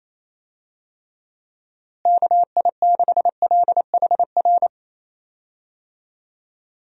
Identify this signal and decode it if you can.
Morse code, heard as a keyed tone, about 28 wpm, 700 Hz. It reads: KI6LHR
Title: USA Call Signs - 6 Characters - Repeated 3x - 28wpm